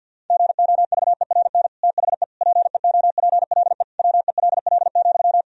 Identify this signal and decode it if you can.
Morse code, heard as a keyed tone, about 49 wpm, 700 Hz. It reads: GOVERN THE PEOPLE WELL,
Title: Sentences from Top 1000 Words - Review Part 2 - 50wpm